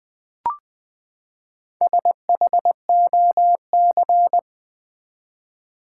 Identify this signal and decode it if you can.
Morse code, heard as a keyed tone, about 20 wpm, 700 Hz. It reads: SHOC
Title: Build A Word - 4 Letters At A Time - 8-Letter Words - 20wpm